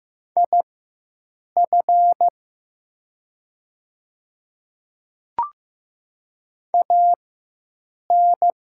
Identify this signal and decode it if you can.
Morse code, heard as a keyed tone, about 15 wpm, spaced as for 7 wpm, 700 Hz. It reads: IF AN
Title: 4x Character Spacing - Top 100 Words - 15wpm